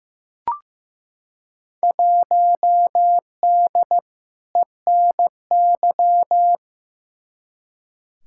Morse code, heard as a keyed tone, about 15 wpm, 700 Hz. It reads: 1D ENY
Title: Field Day Exchanges - 15wpm